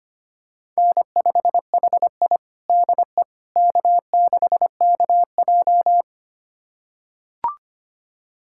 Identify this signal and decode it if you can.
Morse code, heard as a keyed tone, about 25 wpm, 700 Hz. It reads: N5HI DE K6KJ
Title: USA Callsign DE Callsign - 4 Characters - 25wpm